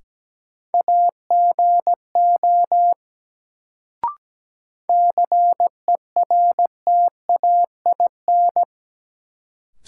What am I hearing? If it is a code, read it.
Morse code, heard as a keyed tone, about 17 wpm, 700 Hz. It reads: AGO CERTAIN
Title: Sentences from Top 500 Words - New Word Drill - 17wpm